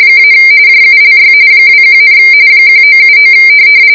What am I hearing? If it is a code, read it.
RTTY, about 45.45 baud. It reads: - JULY 21, 1979